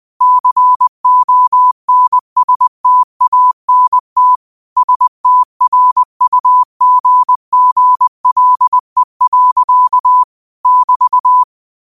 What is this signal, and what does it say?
Morse code, heard as a keyed tone, about 20 wpm, 1000 Hz. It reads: CONSTANT STRUGGLE. <BT>
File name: qotdfiles / 20wpm